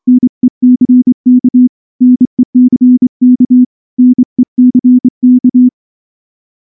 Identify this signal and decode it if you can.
Morse code, heard as a keyed tone, about 25 wpm, 250 Hz. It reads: NECK NECK NECK
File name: morse / morse-training-25WPM